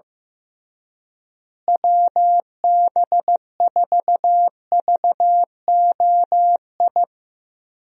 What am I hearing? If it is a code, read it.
Morse code, heard as a keyed tone, about 15 wpm, 700 Hz. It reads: WB4VOI
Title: USA Call Signs - 6 Characters - Repeated 3x - 15wpm